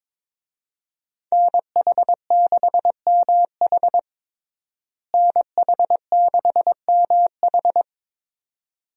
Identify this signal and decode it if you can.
Morse code, heard as a keyed tone, about 22 wpm, 700 Hz. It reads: NH6MH NH6MH
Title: USA Call Signs - 5 Characters - Repeated 2x - 22wpm